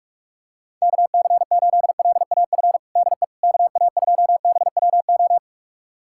Morse code, heard as a keyed tone, about 46 wpm, 700 Hz. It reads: KC8LAF DE KA2BWO
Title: USA Callsign DE Callsign - 6 Characters - 45wpm